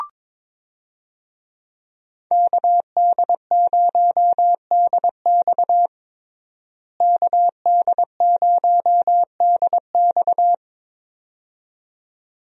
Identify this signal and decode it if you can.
Morse code, heard as a keyed tone, about 22 wpm, 700 Hz. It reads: KD0DX KD0DX
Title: USA Call Signs - 5 Characters - Repeated 2x - 22wpm